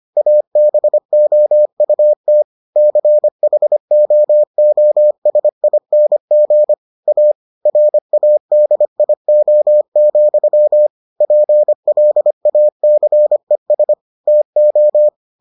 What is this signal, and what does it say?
Morse code, heard as a keyed tone, about 25 wpm, 600 Hz. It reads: ABOUT CHOOSING A RADIO, PLACES TO